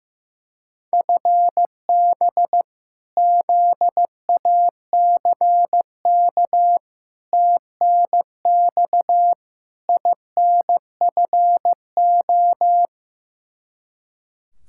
Morse code, heard as a keyed tone, about 16 wpm, 700 Hz. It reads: FB ZACK TNX INFO
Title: Practice QSO - Exchange 3 - More Info - 15wpm